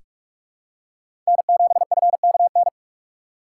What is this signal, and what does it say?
Morse code, heard as a keyed tone, about 43 wpm, 700 Hz. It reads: N7FKN
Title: USA Call Signs - 5 Characters - 2x Character Spacing - 45wpm